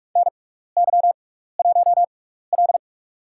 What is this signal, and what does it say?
Morse code, heard as a keyed tone, about 44 wpm, spaced as for 15 wpm, 700 Hz. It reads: NY1L
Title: USA Call Signs - 4 Characters - 6x Character Spacing - 45wpm